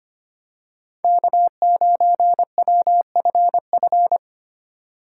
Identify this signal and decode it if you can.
Morse code, heard as a keyed tone, about 27 wpm, 700 Hz. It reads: K9WFF
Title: Speed Racing - USA 1x3 Call Signs 25wpm